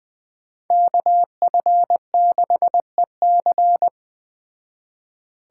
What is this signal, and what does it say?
Morse code, heard as a keyed tone, about 20 wpm, 700 Hz. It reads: KF6EC
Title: Speed Racing - USA Call Signs 17wpm